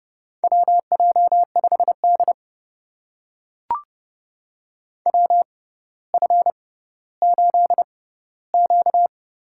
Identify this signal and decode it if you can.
Morse code, heard as a keyed tone, about 30 wpm, spaced as for 15 wpm, 700 Hz. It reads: WJ5D WF8Q
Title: USA Call Signs - 4 Characters - 6x Character Spacing - 30wpm